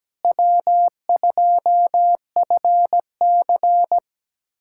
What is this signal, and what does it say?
Morse code, heard as a keyed tone, about 17 wpm, 700 Hz. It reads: W2FC
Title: USA Callsign DE Callsign - 4 Characters - 17wpm